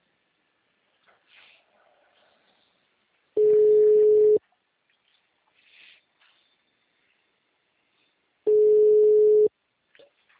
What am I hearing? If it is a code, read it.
Ringback tone (Europe, 425 Hz).